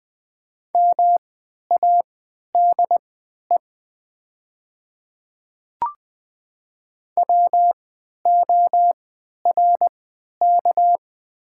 Morse code, heard as a keyed tone, about 20 wpm, spaced as for 11 wpm, 690 Hz. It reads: MADE WORK
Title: 3x Character Spacing - Top 200 Words - 20wpm